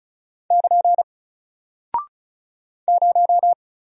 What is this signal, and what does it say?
Morse code, heard as a keyed tone, about 35 wpm, 700 Hz. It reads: <KN> 0
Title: Single Character (Letter:Number:Punctuation:Prosign) Alt 35wpm